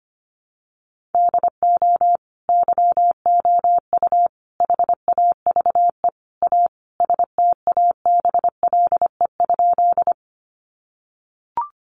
Morse code, heard as a keyed tone, about 25 wpm, 700 Hz. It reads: DO YOU HAVE A STABLE?
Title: Sentences from Top 1500 Words 25wpm